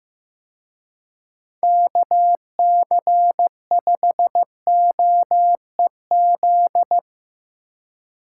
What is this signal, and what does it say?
Morse code, heard as a keyed tone, about 16 wpm, 700 Hz. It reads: KC5OEZ
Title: Speed Racing - USA 2x3 Call Signs 15wpm